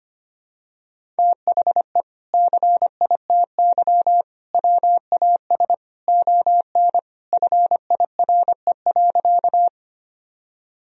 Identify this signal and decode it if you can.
Morse code, heard as a keyed tone, about 25 wpm, 700 Hz. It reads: THE CITY WAS ON FIRE.
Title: Sentences from Top 1000 Words - Review Part 1 - 25wpm